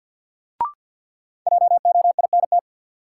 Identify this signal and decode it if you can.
Morse code, heard as a keyed tone, about 52 wpm, 700 Hz. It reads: JOINT
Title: Sets of 5 Words Encore - New Word Drill - 50wpm